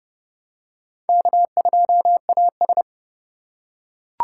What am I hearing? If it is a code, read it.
Morse code, heard as a keyed tone, about 28 wpm, 700 Hz. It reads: K2AS